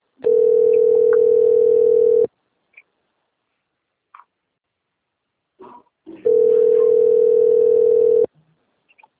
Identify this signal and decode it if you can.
Ringback tone (North America), 440 + 480 Hz.